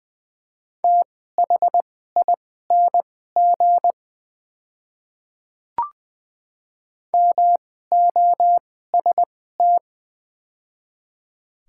Morse code, heard as a keyed tone, about 20 wpm, spaced as for 14 wpm, 700 Hz. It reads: THING MOST
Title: 2x Character Spacing - Top 100 Words - 20wpm